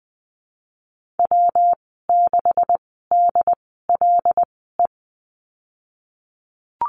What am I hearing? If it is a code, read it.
Morse code, heard as a keyed tone, about 20 wpm, spaced as for 15 wpm, 700 Hz. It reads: W6DLE
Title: 179 Lesson 29g - Slash - Call Signs 20wpm